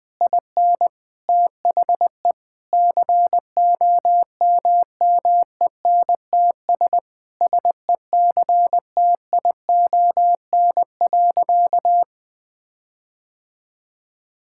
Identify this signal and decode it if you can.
Morse code, heard as a keyed tone, about 20 wpm, 700 Hz. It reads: IN THE COMMENTS SECTION.
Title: Sentences with 9-Word Phrases - Spoken Beginning - 20wpm